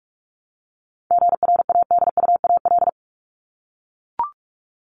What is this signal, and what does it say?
Morse code, heard as a keyed tone, about 44 wpm, 700 Hz. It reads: GRADUAL
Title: LY Suffix - Top 100 Words with Base Words 45wpm